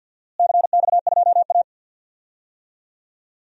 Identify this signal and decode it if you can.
Morse code, heard as a keyed tone, about 53 wpm, 700 Hz. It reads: CX2A
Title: DXCC-100 Call Signs - 4 Characters - Repeated 3x - 50wpm